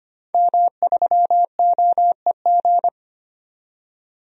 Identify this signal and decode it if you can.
Morse code, heard as a keyed tone, about 25 wpm, 700 Hz. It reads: M3OEG
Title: DXCC-100 Call Signs - 5 Characters - 25wpm